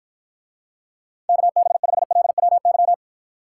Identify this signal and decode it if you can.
Morse code, heard as a keyed tone, about 53 wpm, 700 Hz. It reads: KB5LWY